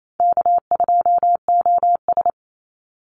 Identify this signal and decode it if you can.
Morse code, heard as a keyed tone, about 28 wpm, 700 Hz. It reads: K2OS
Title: Easy Call Signs Repeated 3x - 28wpm